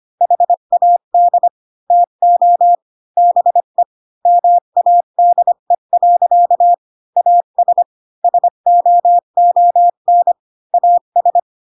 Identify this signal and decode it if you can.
Morse code, heard as a keyed tone, about 25 wpm, 700 Hz. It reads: HAD TO BE MADE. AS SOON AS
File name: Pinocchio / Pinocchio ch3 20 25 ik1hge